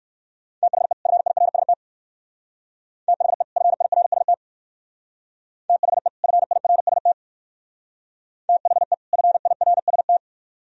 Morse code, heard as a keyed tone, about 64 wpm, 700 Hz. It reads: THE FIRST THE FIRST THE FIRST THE FIRST